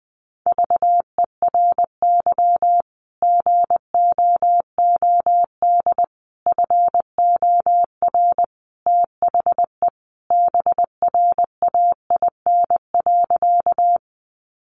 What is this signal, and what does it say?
Morse code, heard as a keyed tone, about 20 wpm, 700 Hz. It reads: VERY GOOD FOR THE BRAIN.
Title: Sentences from Top 1100 Words 20wpm